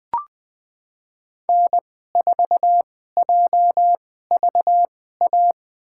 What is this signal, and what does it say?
Morse code, heard as a keyed tone, about 20 wpm, spaced as for 14 wpm, 700 Hz. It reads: N4JVA
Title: USA Call Signs - 5 Characters - 2x Character Spacing - 20wpm